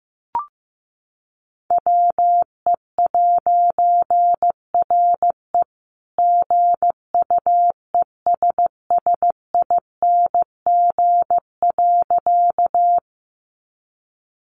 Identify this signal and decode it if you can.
Morse code, heard as a keyed tone, about 15 wpm, 700 Hz. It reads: WE'RE GUESSING.
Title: A Sound Of Thunder 15wpm